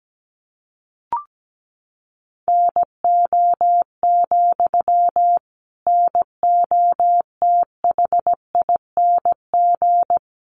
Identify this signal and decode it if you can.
Morse code, heard as a keyed tone, about 17 wpm, 700 Hz. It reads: NO, NOTHING
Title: Sentences from Top 800 Words 17wpm